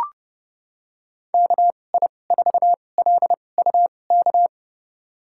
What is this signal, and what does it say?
Morse code, heard as a keyed tone, about 31 wpm, spaced as for 22 wpm, 700 Hz. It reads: KI4LUK
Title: USA Call Signs - 2x Character Spacing - 30wpm